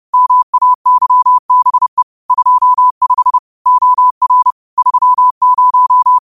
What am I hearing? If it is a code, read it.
Morse code, heard as a keyed tone, about 30 wpm, 1000 Hz. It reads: MAYBE 25 OR 30